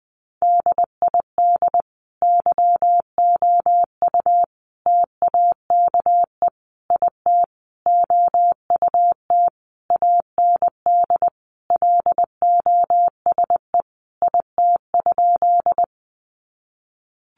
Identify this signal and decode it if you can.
Morse code, heard as a keyed tone, about 19 wpm, 700 Hz. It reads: DID YOU TAKE IT OUT AND LOSE IT?
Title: Sentences from Top 1100 Words 20wpm